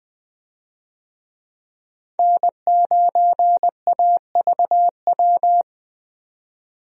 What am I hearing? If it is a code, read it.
Morse code, heard as a keyed tone, about 20 wpm, 700 Hz. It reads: N9AVW
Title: USA Call Signs - 5 Characters - 20wpm